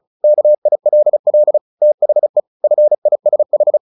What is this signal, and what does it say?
Morse code, heard as a keyed tone, about 34 wpm, 600 Hz. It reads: KILL THE FISH
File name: the old man and the sea 2 35wpm